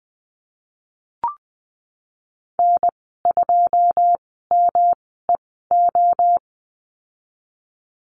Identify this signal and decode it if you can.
Morse code, heard as a keyed tone, about 19 wpm, spaced as for 14 wpm, 700 Hz. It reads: N2MEO
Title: 128 Lesson 21g - Number 6 - Call Signs 20wpm